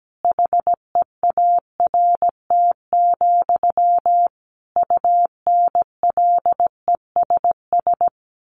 Morse code, heard as a keyed tone, about 17 wpm, 700 Hz. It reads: HEART, UNLESS
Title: Sentences from Top 1500 Words 17wpm